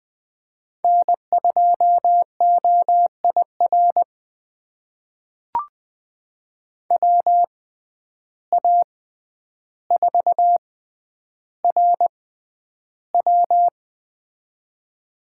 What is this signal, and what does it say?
Morse code, heard as a keyed tone, about 20 wpm, spaced as for 10 wpm, 700 Hz. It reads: N2OIR WA4RW
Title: USA Call Signs - 5 Characters - 6x Character Spacing - 20wpm